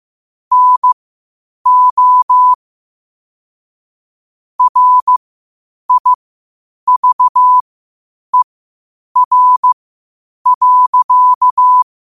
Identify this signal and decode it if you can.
Morse code, heard as a keyed tone, about 15 wpm, spaced as for 8 wpm, 1000 Hz. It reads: NO RIVER.
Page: New quotes every day in morse code at 8 Words per minute.